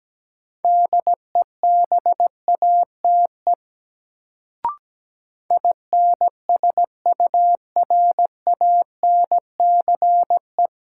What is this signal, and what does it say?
Morse code, heard as a keyed tone, about 17 wpm, 700 Hz. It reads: DEBATE INSURANCE
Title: Sentences from Top 1100 Words - New Word Drill - 17wpm